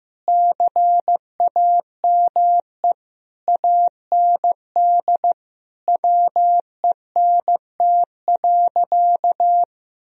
Morse code, heard as a keyed tone, about 15 wpm, 700 Hz. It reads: CAME AND WENT.
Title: Sentences from Top 1000 Words - Review Part 2 - 15wpm